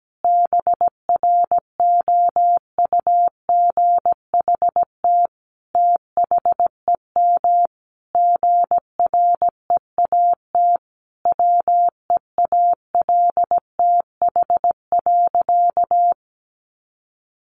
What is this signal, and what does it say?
Morse code, heard as a keyed tone, about 17 wpm, 700 Hz. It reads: BROUGHT THEM GREAT WEALTH.
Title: Sentences from Top 1300 Words 17wpm